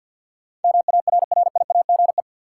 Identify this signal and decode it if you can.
Morse code, heard as a keyed tone, about 47 wpm, 700 Hz. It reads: MARRIAGE
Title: Sentences from Top 800 Words - New Word Drill - 50wpm